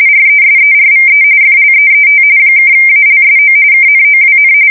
RTTY, about 45.45 baud. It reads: RESERVE FLEET (ALSO KNOWN
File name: fsk170 / USS-Missouri